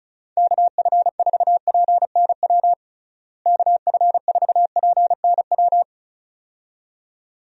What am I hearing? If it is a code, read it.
Morse code, heard as a keyed tone, about 36 wpm, 700 Hz. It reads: KF4PNW KF4PNW
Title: USA Call Signs - 6 Characters - Repeated 2x - 35wpm